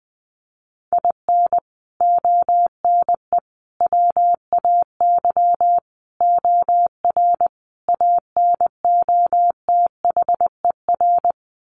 Morse code, heard as a keyed tone, about 20 wpm, 700 Hz. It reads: IN ONE WAY OR ANOTHER
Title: Sets of 5 Words Encore 20wpm